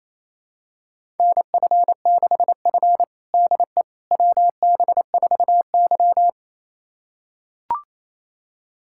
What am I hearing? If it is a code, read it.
Morse code, heard as a keyed tone, about 28 wpm, 700 Hz. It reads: NF6F DE WB4Y
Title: USA Callsign DE Callsign - 4 Characters - 28wpm